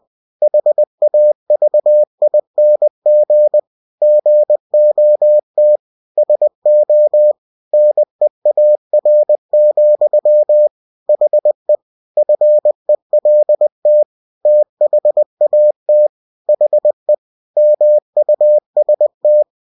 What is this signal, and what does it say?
Morse code, heard as a keyed tone, about 20 wpm, 600 Hz. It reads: HAVING GOT SO NEAR, HE FELT THAT HE MUST
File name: cw-books / winnie-the-pooh / winnie-20wpm-page-0042